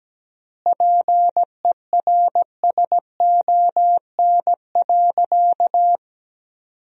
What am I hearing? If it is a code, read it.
Morse code, heard as a keyed tone, about 17 wpm, 700 Hz. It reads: PERSON.
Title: 4x Word-Spacing - Sentences from Top 500 Words - 17wpm